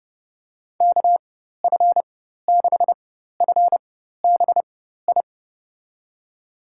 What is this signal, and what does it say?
Morse code, heard as a keyed tone, about 30 wpm, spaced as for 14 wpm, 700 Hz. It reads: KF6FBI
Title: USA Call Signs - 6 Characters - 4x Character Spacing - 30wpm